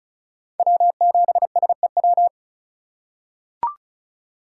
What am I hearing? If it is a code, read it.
Morse code, heard as a keyed tone, about 36 wpm, 700 Hz. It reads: W7SEW
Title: USA Call Signs - 5 Characters - 2x Character Spacing - 35wpm